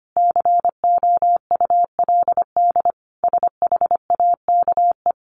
Morse code, heard as a keyed tone, about 25 wpm, 700 Hz. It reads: COULD SHAKE
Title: A Sound Of Thunder 25wpm